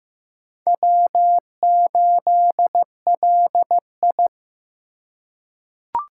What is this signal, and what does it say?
Morse code, heard as a keyed tone, about 15 wpm, 700 Hz. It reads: W8LI
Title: USA Call Signs - 4x Character Spacing - 15wpm